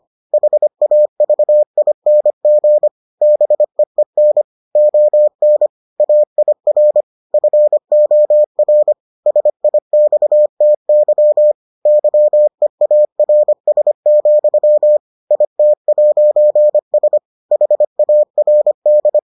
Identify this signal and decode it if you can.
Morse code, heard as a keyed tone, about 25 wpm, 600 Hz. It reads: HAVING BEEN ON AIR FOR SIXTY YEARS, IT'S HARD